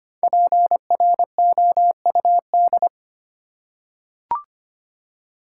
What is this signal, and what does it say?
Morse code, heard as a keyed tone, about 25 wpm, 700 Hz. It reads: PROUD
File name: 5 Letter Words - First 3 Letters Given - 25wpm